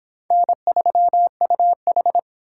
Morse code, heard as a keyed tone, about 30 wpm, 700 Hz. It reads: N3UH